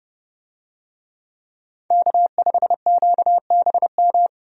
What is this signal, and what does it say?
Morse code, heard as a keyed tone, about 29 wpm, 700 Hz. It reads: K5QBM
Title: USA Call Signs - 5 Characters - 30wpm